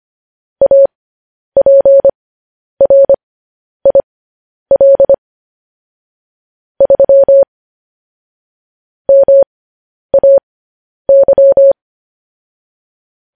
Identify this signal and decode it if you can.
Morse code, heard as a keyed tone, about 25 wpm, spaced as for 10 wpm, 550 Hz. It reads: APRIL 3 MAY